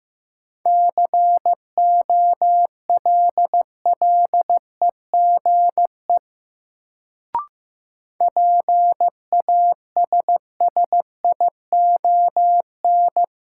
Morse code, heard as a keyed tone, about 15 wpm, 700 Hz. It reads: COLLEGE PASSION
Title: Sentences from Top 1300 Words - New Word Drill Alt - 15wpm